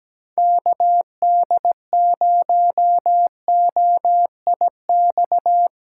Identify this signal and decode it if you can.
Morse code, heard as a keyed tone, about 17 wpm, 700 Hz. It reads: KD0OIX
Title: K1USN SST QSO Practice - 17wpm